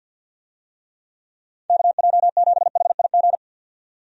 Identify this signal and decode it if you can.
Morse code, heard as a keyed tone, about 52 wpm, 700 Hz. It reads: KJ7HIG
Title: USA Call Signs - 6 Characters - 50wpm